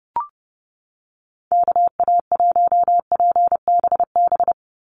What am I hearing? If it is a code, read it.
Morse code, heard as a keyed tone, about 30 wpm, 700 Hz. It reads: KA1PBB
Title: USA Call Signs 30wpm